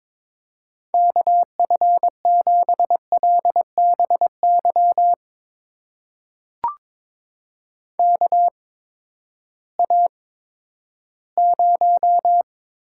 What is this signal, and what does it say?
Morse code, heard as a keyed tone, about 21 wpm, spaced as for 12 wpm, 700 Hz. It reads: KF7LBY KA0
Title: USA Call Signs - 6 Characters - 8x Character Spacing - 22wpm